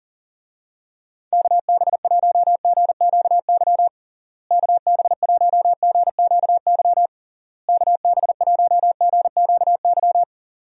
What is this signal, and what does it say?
Morse code, heard as a keyed tone, about 40 wpm, 700 Hz. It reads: KB1GQY KB1GQY KB1GQY
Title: USA Call Signs - 6 Characters - Repeated 3x - 40wpm